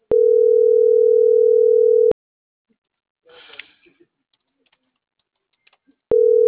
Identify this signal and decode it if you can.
Ringback tone (North America), 440 + 480 Hz.
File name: CENTRAL MAERSK / out polanco.msk